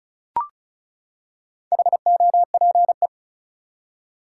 Morse code, heard as a keyed tone, about 34 wpm, 700 Hz. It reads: HOPE
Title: Four Letter Words - 35wpm